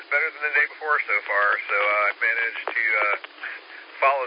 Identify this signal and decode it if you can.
Emergency Alert System (SAME) tones, heard as harsh buzzing bursts, with speech. It is heard from 0.7 to 3.9 s.